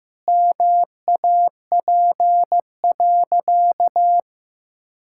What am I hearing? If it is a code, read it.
Morse code, heard as a keyed tone, about 16 wpm, 700 Hz. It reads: MAP.
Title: Sentences from Top 1000 Words - Review Part 6 - 15wpm